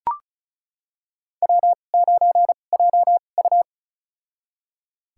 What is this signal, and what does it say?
Morse code, heard as a keyed tone, about 36 wpm, spaced as for 25 wpm, 700 Hz. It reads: W9JU
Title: USA Call Signs - 2x Character Spacing - 35wpm